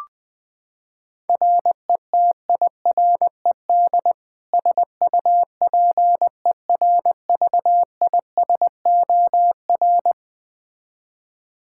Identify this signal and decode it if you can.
Morse code, heard as a keyed tone, about 20 wpm, 700 Hz. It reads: RETIRED SUPERVISOR
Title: K1USN SST QSO Practice 20wpm